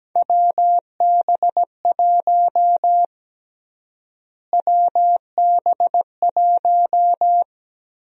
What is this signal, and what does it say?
Morse code, heard as a keyed tone, about 17 wpm, 700 Hz. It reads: WB1 WB1
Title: USA Call Sign Prefixes - Repeated 2x - 17wpm